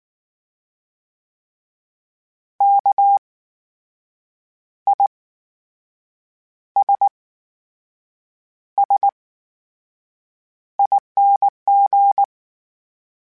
Morse code, heard as a keyed tone, about 19 wpm, spaced as for 6 wpm, 800 Hz. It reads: KISSING
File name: Special INGs / Special INGs-mp3s / VerySlowToFaster Ing @18